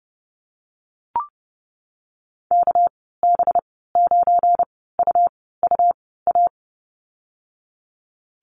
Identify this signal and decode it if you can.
Morse code, heard as a keyed tone, about 30 wpm, spaced as for 17 wpm, 700 Hz. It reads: KB9UUA
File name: Lesson 37g - Letter x - Call Signs 30wpm